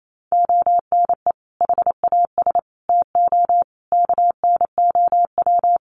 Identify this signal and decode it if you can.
Morse code, heard as a keyed tone, about 28 wpm, 700 Hz. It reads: ONE HAS TO KNOW
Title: Sentences from Top 800 Words 28wpm